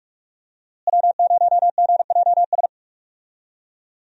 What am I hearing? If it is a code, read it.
Morse code, heard as a keyed tone, about 43 wpm, 700 Hz. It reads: W0GJS
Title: USA Call Signs - 5 Characters - Repeated 3x - 45wpm